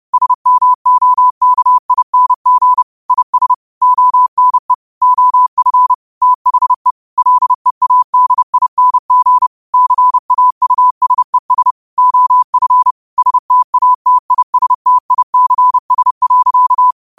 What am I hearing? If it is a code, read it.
Morse code, heard as a keyed tone, about 30 wpm, 1000 Hz. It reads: SMOKING IS ONE OF THE LEADING CAUSES OF STATISTICS.
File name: qotdfiles / 30wpm